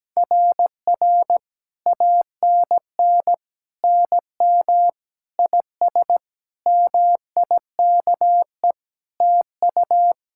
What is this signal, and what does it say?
Morse code, heard as a keyed tone, about 17 wpm, 700 Hz. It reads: RR ANN NM IS MIKE TU
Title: SOTA QSO Practice 17wpm